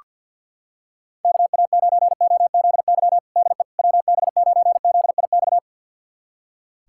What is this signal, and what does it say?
Morse code, heard as a keyed tone, about 46 wpm, 690 Hz. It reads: KA9OZY DE WB9ZIX
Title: USA Callsign DE Callsign - 6 Characters - 50wpm